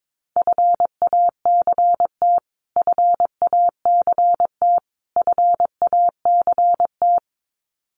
Morse code, heard as a keyed tone, about 22 wpm, 700 Hz. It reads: FACT FACT FACT
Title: Top 500 Words Repeated 22wpm